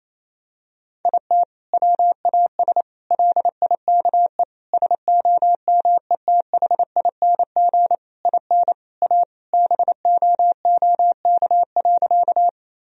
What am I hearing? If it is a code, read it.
Morse code, heard as a keyed tone, about 28 wpm, 700 Hz. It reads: IT WAS LIKE SOMETHING IN A BOOK.
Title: Sentences from Top 1000 Words - Review Part 1 - 28wpm